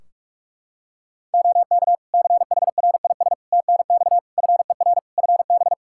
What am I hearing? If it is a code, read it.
Morse code, heard as a keyed tone, about 44 wpm, 700 Hz. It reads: OK CHRIS TNX FER FB
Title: Practice QSO - Exchange 4 - Ending - 45wpm